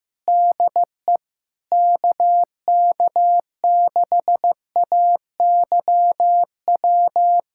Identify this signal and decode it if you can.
Morse code, heard as a keyed tone, about 15 wpm, 700 Hz. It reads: DE KK6AYW
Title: USA Callsign DE Callsign - 6 Characters - 15wpm